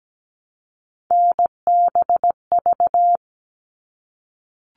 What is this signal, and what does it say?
Morse code, heard as a keyed tone, about 17 wpm, 700 Hz. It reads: NBV
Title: All Cut Numbers - 3-Digit 17wpm